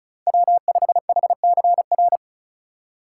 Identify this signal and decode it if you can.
Morse code, heard as a keyed tone, about 36 wpm, 700 Hz. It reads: W5HCR
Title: USA Call Signs - 5 Characters - 2x Character Spacing - 35wpm